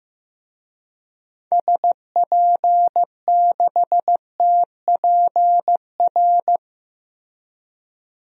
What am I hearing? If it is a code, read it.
Morse code, heard as a keyed tone, about 15 wpm, 690 Hz. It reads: SP6TPR
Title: DXCC-100 Call Signs - 6 Characters - 15wpm